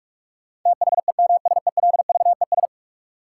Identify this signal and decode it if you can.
Morse code, heard as a keyed tone, about 46 wpm, 700 Hz. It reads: THEMSELVES
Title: Sentences from Top 800 Words - New Word Drill - 45wpm